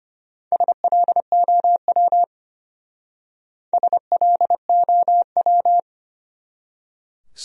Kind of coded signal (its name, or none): Morse code